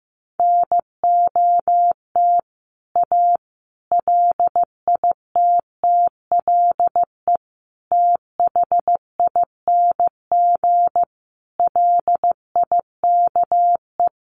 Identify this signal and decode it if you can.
Morse code, heard as a keyed tone, about 15 wpm, 700 Hz. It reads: NOT A LITTLE THING LIKE
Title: A Sound Of Thunder 15wpm